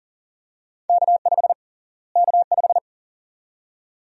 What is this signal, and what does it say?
Morse code, heard as a keyed tone, about 40 wpm, 700 Hz. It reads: K5 K5
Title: USA Call Sign Prefixes - Repeated 2x - 40wpm